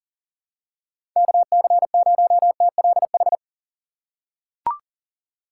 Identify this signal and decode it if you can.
Morse code, heard as a keyed tone, about 38 wpm, 700 Hz. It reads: KC0TLH
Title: USA Call Signs - 6 Characters - 3x Character Spacing - 40wpm